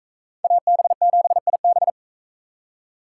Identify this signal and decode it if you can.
Morse code, heard as a keyed tone, about 42 wpm, 700 Hz. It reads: AB7IB